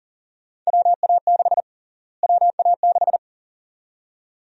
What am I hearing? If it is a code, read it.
Morse code, heard as a keyed tone, about 40 wpm, 700 Hz. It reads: WA6 WA6
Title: USA Call Sign Prefixes - Repeated 2x - 40wpm